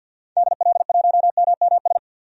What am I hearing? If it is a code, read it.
Morse code, heard as a keyed tone, about 51 wpm, 700 Hz. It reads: DL1MMS
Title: DXCC-100 Call Signs - 6 Characters - 50wpm